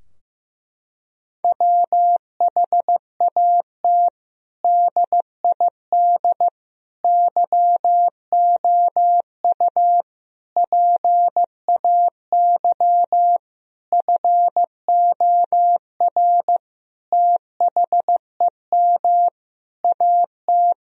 Morse code, heard as a keyed tone, about 15 wpm, 700 Hz. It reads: WHAT DID YOU PAY FOR THEM AT
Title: Sentences from Top 1000 Words - Review Part 3 - 15wpm